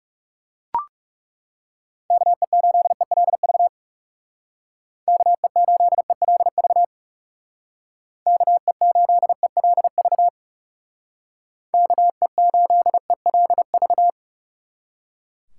Morse code, heard as a keyed tone, about 37 wpm, 700 Hz. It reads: KE8ELV KE8ELV KE8ELV KE8ELV
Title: Speed Racing - USA 2x3 Call Signs 30wpm